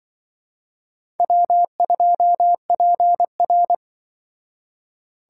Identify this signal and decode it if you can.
Morse code, heard as a keyed tone, about 24 wpm, 700 Hz. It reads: W2PR
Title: Speed Racing - USA 1x2 Call Signs 20wpm